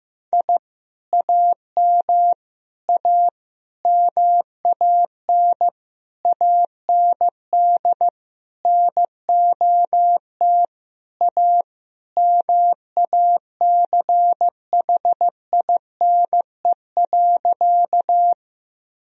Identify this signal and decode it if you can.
Morse code, heard as a keyed tone, about 15 wpm, 700 Hz. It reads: I AM A MAN AND NOT A MACHINE.
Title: Sentences from Top 1000 Words - Review Part 2 - 15wpm